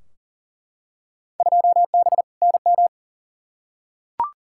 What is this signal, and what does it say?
Morse code, heard as a keyed tone, about 40 wpm, 700 Hz. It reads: 2B NM
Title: Field Day Exchanges Encore - 40wpm